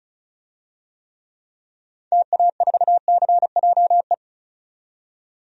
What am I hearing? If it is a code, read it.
Morse code, heard as a keyed tone, about 34 wpm, 700 Hz. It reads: TA4CJE
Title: DXCC-100 Call Signs - 6 Characters - 35wpm